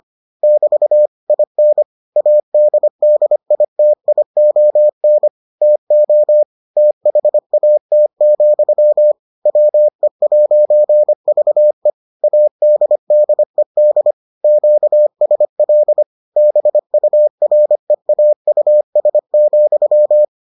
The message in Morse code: <BT> IN ADDITION TO THAT, WE'VE ADDED QSL BUREAUS,

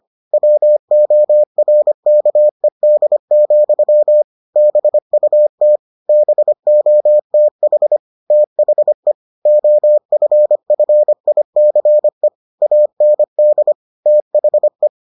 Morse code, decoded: WORKED, BUT BOTH THE OFFICE AND THE

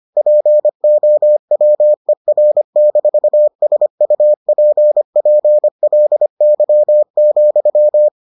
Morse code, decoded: POWER-SUPPLY,

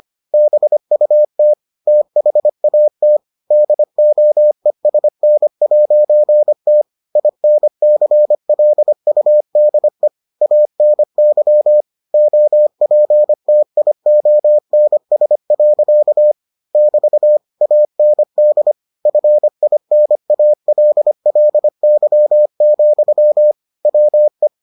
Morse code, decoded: BUT THAT DOESN'T INCLUDE ANY OPTIONS. <BT> AND FINALLY, WE

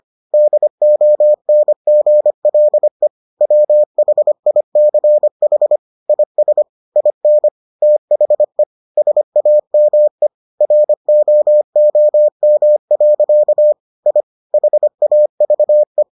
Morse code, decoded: DONGLE WHICH IS IN THE SAME ROOM. I HAVE

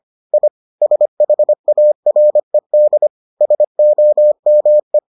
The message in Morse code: I SHARED SOME

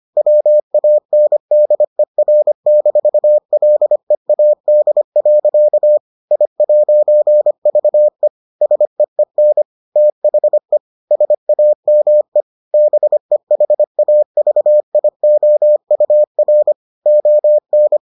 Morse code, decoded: WANDER-LEAD. I'VE SEEN THE SAME BEHAVIOUR ON

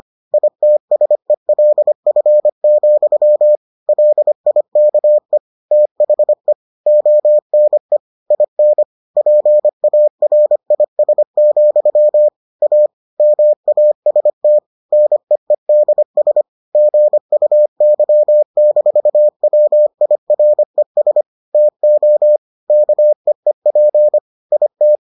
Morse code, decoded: ITSELF, LIKE THE ONE IN PARIS, A MAST NEEDS GUY-WIRES TO KEEP IT

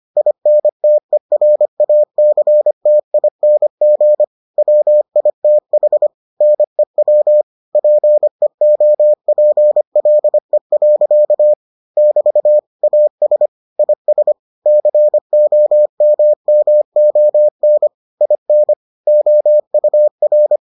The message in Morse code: INTERACTING WITH NEW PEOPLE. <BT> AS IS COMMON IN OUR

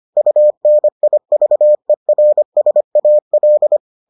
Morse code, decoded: UNIVERSAL

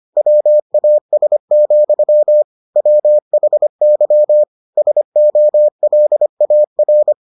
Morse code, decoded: WAS, WHY SOLAR

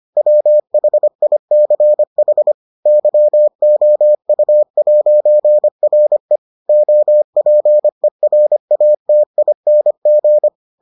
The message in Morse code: WHICH YOU'RE OPERATING

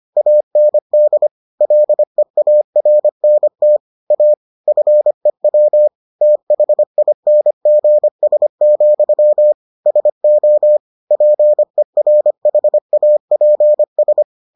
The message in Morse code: AND LEARNT A FEW THINGS, SO PERHAPS

